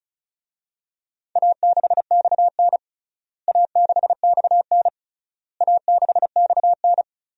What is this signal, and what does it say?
Morse code, heard as a keyed tone, about 36 wpm, 700 Hz. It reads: A6XN A6XN A6XN
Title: DXCC-100 Call Signs - 4 Characters - Repeated 3x - 35wpm